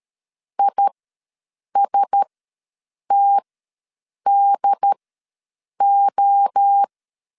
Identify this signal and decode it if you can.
Morse code, heard as a keyed tone, about 13 wpm, spaced as for 7 wpm, 800 Hz. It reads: ISTDO
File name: LEZIONI / 5 LBVC / 5 LBV ws 12wpm 3015